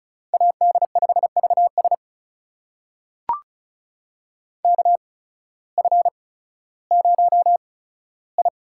Morse code, heard as a keyed tone, about 36 wpm, spaced as for 16 wpm, 700 Hz. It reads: AD5VS KF0I
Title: USA Call Signs - 5 Characters - 8x Character Spacing - 35wpm